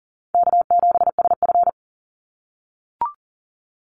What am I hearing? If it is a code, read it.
Morse code, heard as a keyed tone, about 40 wpm, 700 Hz. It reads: K7SF
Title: Easy Call Signs - 40wpm